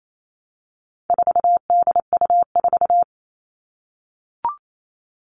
Morse code, heard as a keyed tone, about 28 wpm, 700 Hz. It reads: 4DU4
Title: All Cut Numbers - 4-Digit 28wpm